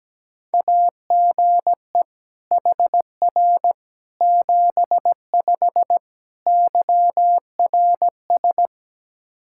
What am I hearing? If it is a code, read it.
Morse code, heard as a keyed tone, about 17 wpm, 700 Hz. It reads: AGE HR 75 YRS
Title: K1USN SST QSO Practice 17wpm